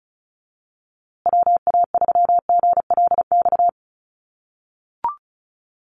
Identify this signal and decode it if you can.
Morse code, heard as a keyed tone, about 34 wpm, 700 Hz. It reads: WA3GLX
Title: USA Call Signs Repeated 2x 35wpm